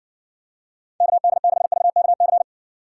Morse code, heard as a keyed tone, about 60 wpm, 700 Hz. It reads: KD6VKC